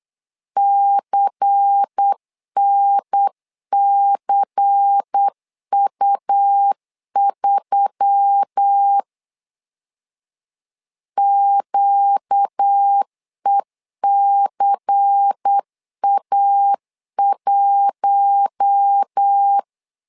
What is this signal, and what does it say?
Morse code, heard as a keyed tone, about 8 wpm, 800 Hz. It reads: CNCU3 QECA1
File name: LEZIONI / 5 LBVC / 5 LBVC 8wpm 2935